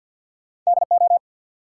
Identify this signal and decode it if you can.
Morse code, heard as a keyed tone, about 52 wpm, 700 Hz. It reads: DO
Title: Sets of 2 Words - New Word Drill - 50wpm